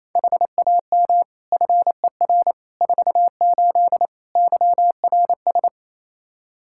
Morse code, heard as a keyed tone, about 28 wpm, 700 Hz. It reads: HAM FER 48 YRS
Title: Practice QSO - Exchange 3 - More Info - 28wpm